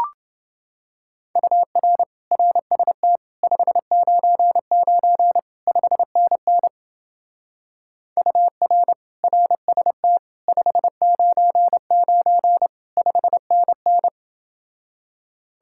Morse code, heard as a keyed tone, about 29 wpm, 700 Hz. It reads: UR RST 599 5NN UR RST 599 5NN